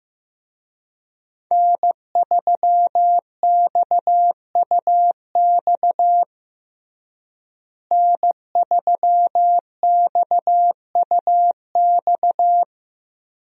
Morse code, heard as a keyed tone, about 15 wpm, 700 Hz. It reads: N3XUX N3XUX
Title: USA Call Signs - 5 Characters - Repeated 2x - 15wpm